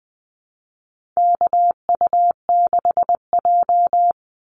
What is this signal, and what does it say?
Morse code, heard as a keyed tone, about 20 wpm, 700 Hz. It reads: KU6J
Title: NA SOTA Activator Callsigns 20wpm